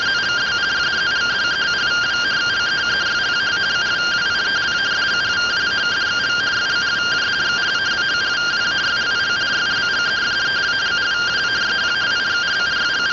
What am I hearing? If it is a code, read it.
Two-tone teleprinter signal, about 170 Hz shift.